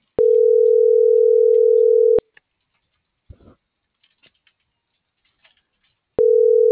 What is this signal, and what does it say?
Ringback tone (North America), 440 + 480 Hz.